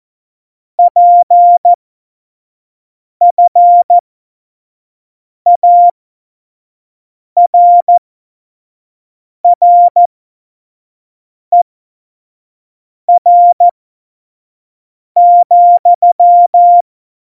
Morse code, heard as a keyed tone, about 14 wpm, spaced as for 5 wpm, 700 Hz. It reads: PFARRER,
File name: Morsekurs / Witze / Witz 08. Dezember2020, 25 F70